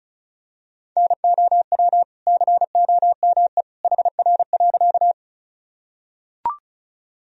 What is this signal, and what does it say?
Morse code, heard as a keyed tone, about 35 wpm, 700 Hz. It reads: NOW COME HR.
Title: Sentences from Top 200 Words and Abbreviations 35wpm